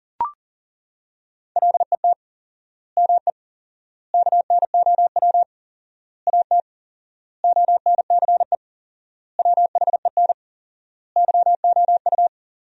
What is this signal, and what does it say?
Morse code, heard as a keyed tone, about 37 wpm, 700 Hz. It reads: LET ME KNOW AT ONCE WHEN YOU
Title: 4x Word-Spacing - Sentences from Top 300 Words - 40wpm